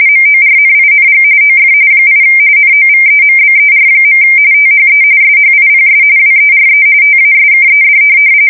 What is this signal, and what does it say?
RTTY, about 45.45 baud. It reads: MORNING OF 8 NOVEMBER. THE SHIPS WERE TASKED